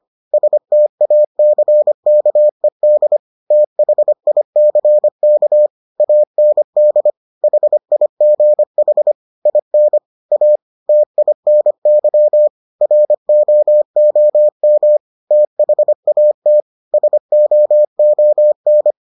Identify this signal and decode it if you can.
Morse code, heard as a keyed tone, about 25 wpm, 600 Hz. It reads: STACKED THICK AND HIGH IN A TINY ROOM THAT SOON